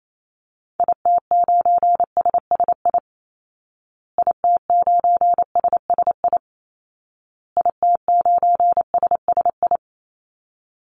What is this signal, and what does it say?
Morse code, heard as a keyed tone, about 28 wpm, 700 Hz. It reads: IT9SSI IT9SSI IT9SSI
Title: Call Signs Repeated 2x - 28wpm